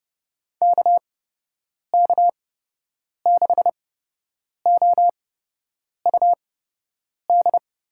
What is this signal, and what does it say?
Morse code, heard as a keyed tone, about 30 wpm, spaced as for 8 wpm, 700 Hz. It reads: KK6OUD
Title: USA Call Signs - 6 Characters - 8x Character Spacing - 30wpm